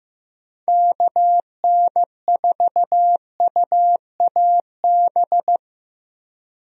Morse code, heard as a keyed tone, about 15 wpm, 700 Hz. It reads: KN4UAB
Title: USA Call Signs - 6 Characters - Repeated 2x - 15wpm